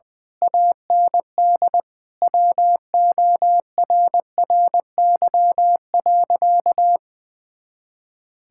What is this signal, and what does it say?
Morse code, heard as a keyed tone, about 20 wpm, 700 Hz. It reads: AND WORRY.
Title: Pangram Challenge v2 - 20wpm